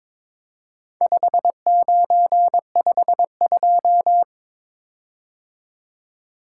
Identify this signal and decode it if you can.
Morse code, heard as a keyed tone, about 22 wpm, 700 Hz. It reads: 5952
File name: practice / 22wpm / Four Letter-Number Combinations - 22wpm